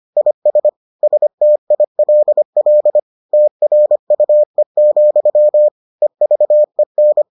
Morse code, heard as a keyed tone, about 25 wpm, 600 Hz. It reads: IS STILL TRUE, EVEN